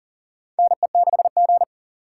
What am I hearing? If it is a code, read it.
Morse code, heard as a keyed tone, about 38 wpm, 700 Hz. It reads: NE6G
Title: USA Call Signs - 4 Characters - 4x Character Spacing - 40wpm